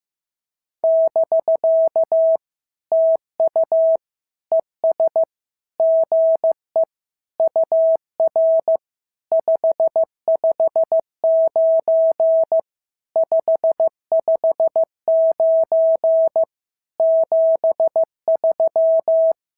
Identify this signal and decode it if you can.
Morse code, heard as a keyed tone, about 14 wpm, 650 Hz. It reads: <BK> TU ES GE UR 559 559 73
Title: SOTA QSO Practice 15wpm